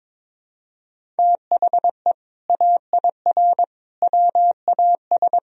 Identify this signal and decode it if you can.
Morse code, heard as a keyed tone, about 22 wpm, 700 Hz. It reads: THE AIR WAS
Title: Sentences from Top 1000 Words - Review Part 3 - 22wpm